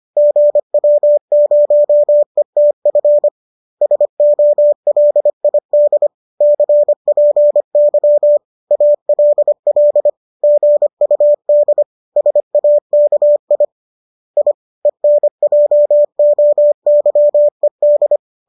Morse code, heard as a keyed tone, about 25 wpm, 590 Hz. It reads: GW0ETF SOLID CPY ALL GUD SAKI I ENJOYED